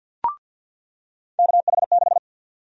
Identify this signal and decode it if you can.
Morse code, heard as a keyed tone, about 48 wpm, 700 Hz. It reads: KH6
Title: USA Call Sign Prefixes - 50wpm